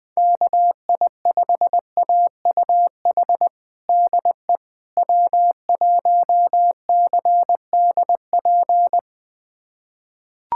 Morse code, heard as a keyed tone, about 21 wpm, 700 Hz. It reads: KI5AUH DE W1CDP
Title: USA Callsign DE Callsign 20wpm